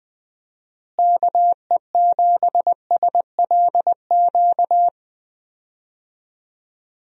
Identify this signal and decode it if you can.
Morse code, heard as a keyed tone, about 20 wpm, 700 Hz. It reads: KE7SLQ
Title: USA Call Signs - 6 Characters - Repeated 3x - 20wpm